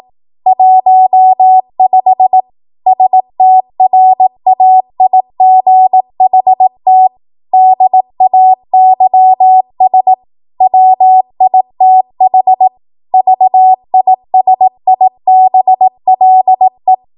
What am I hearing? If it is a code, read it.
Morse code, heard as a keyed tone, about 18 wpm, 750 Hz. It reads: 15 STRAIGHT DAYS WITH VISIBLE
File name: Archive / 18 WPM / 18WPM